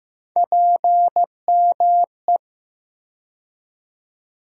Morse code, heard as a keyed tone, about 15 wpm, 700 Hz. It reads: PME
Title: USA Call Sign Postfixes - 3 Letters 15wpm